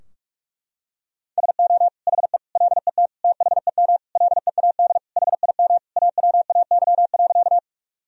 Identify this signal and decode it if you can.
Morse code, heard as a keyed tone, about 46 wpm, 700 Hz. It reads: SO HE LET THEM LEAD HIM AWAY.
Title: Sentences from Top 1000 Words - Review Part 2 - 45wpm